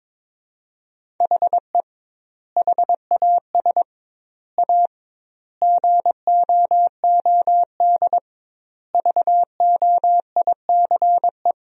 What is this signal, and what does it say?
Morse code, heard as a keyed tone, about 22 wpm, 700 Hz. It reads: HE HAS A GOOD VOICE
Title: 2x Word-Spacing - Sentences from Top 500 Words - 22wpm